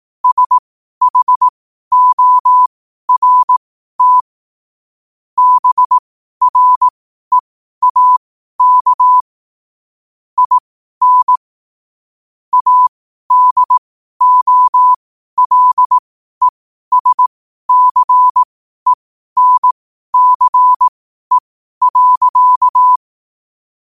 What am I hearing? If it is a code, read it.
Morse code, heard as a keyed tone, about 18 wpm, spaced as for 12 wpm, 1000 Hz. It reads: SHORT BREAK IN ADOLESCENCE.
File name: qotdfiles / 12wpm